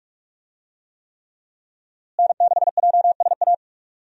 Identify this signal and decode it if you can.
Morse code, heard as a keyed tone, about 46 wpm, 700 Hz. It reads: N6JSA
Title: USA Call Signs - 5 Characters - Repeated 3x - 45wpm